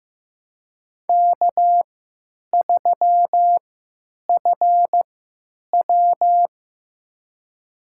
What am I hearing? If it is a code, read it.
Morse code, heard as a keyed tone, about 15 wpm, spaced as for 9 wpm, 700 Hz. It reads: K3FW
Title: USA Call Signs - 4 Characters - 3x Character Spacing - 15wpm